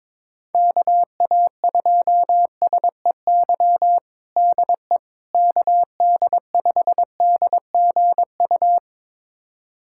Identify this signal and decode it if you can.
Morse code, heard as a keyed tone, about 22 wpm, 700 Hz. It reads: KA2SEY DE KD5DGU
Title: USA Callsign DE Callsign 22wpm